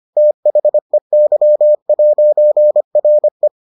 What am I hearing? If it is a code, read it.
Morse code, heard as a keyed tone, about 25 wpm, 600 Hz. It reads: THEY'RE